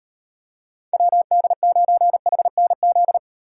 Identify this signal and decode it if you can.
Morse code, heard as a keyed tone, about 38 wpm, 700 Hz. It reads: WD9HNZ